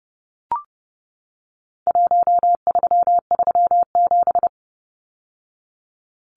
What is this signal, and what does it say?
Morse code, heard as a keyed tone, about 30 wpm, 700 Hz. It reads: 1337
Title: Numbers - 4-Digit 30wpm